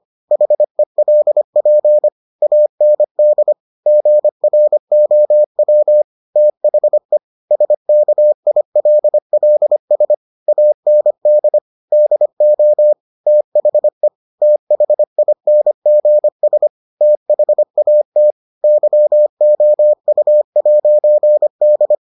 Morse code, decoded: HELP AND GROW THE SKILLS AND DO THE THINGS THAT YOU'D

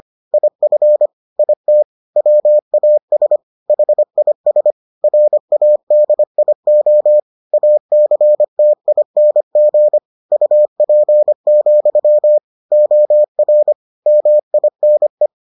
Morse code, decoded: IF IT WAS HIS RADIO ACTING UP, OR MINE

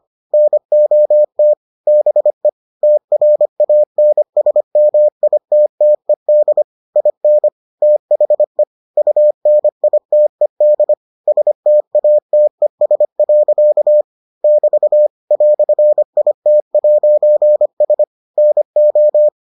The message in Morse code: NOT BE TRANSMITTED IN THE UNITED STATES. <BT> "IT'S NO

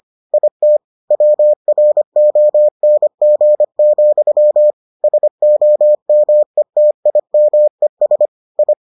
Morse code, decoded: IT WRONG, SOMETIMES I

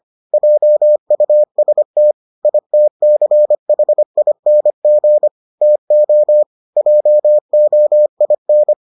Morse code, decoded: JUST ITCHING TO JOIN

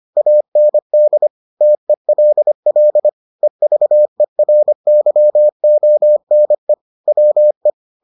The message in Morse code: AND TELL EVERYONE WE

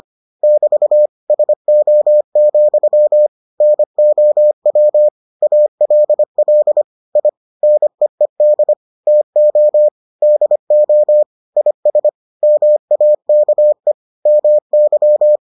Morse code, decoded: <BT> SO, NOW ALL I NEED TO DO IS MAKE MY

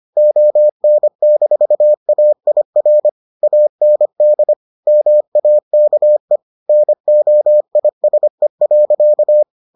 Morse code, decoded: ON-AIR AND MAKE NOISE.

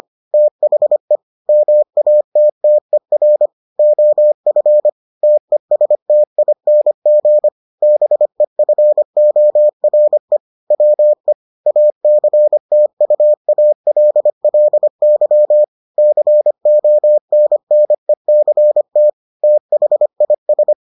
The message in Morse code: THE MATTER OF TESTING BEFORE WE ACTUALLY CONNECT THIS

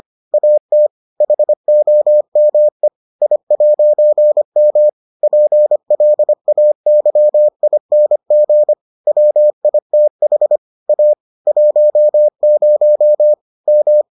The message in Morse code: AT HOME I'M PLAYING WITH A 10 M

